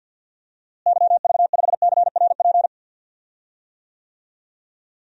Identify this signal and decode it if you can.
Morse code, heard as a keyed tone, about 53 wpm, 700 Hz. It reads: YV5CRP
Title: DXCC-100 Call Signs - 6 Characters - Repeated 3x - 50wpm